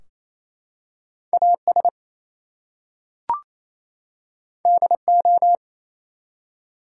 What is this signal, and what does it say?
Morse code, heard as a keyed tone, about 27 wpm, 700 Hz. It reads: AS DO
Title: Two Letter Words - 28wpm